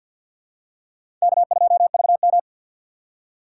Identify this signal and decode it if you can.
Morse code, heard as a keyed tone, about 51 wpm, 700 Hz. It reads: K2VM
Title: USA Call Signs - 3x Character Spacing - 50wpm